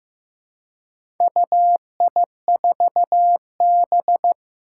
Morse code, heard as a keyed tone, about 15 wpm, 700 Hz. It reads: UI4B
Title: Four Challenging Letter-Number Combinations 15wpm